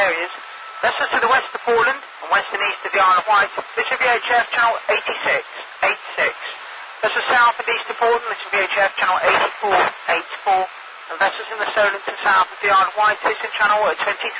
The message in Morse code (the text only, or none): none